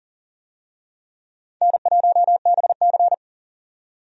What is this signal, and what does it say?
Morse code, heard as a keyed tone, about 40 wpm, 700 Hz. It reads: N1BC
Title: USA Call Signs - 4 Characters - 40wpm